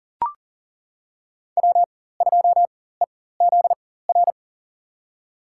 Morse code, heard as a keyed tone, about 40 wpm, spaced as for 19 wpm, 700 Hz. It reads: W2EZR
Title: USA Call Signs - 5 Characters - 4x Character Spacing - 40wpm